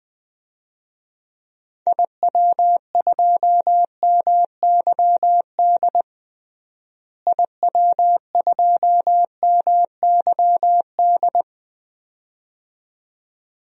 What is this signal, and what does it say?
Morse code, heard as a keyed tone, about 21 wpm, 700 Hz. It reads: IW2MYD IW2MYD
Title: DXCC-100 Call Signs - 6 Characters - Repeated 2x - 20wpm